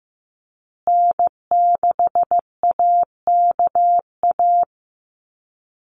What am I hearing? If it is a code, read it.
Morse code, heard as a keyed tone, about 15 wpm, 700 Hz. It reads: N6AKA
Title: USA Call Signs Repeated 2x 15wpm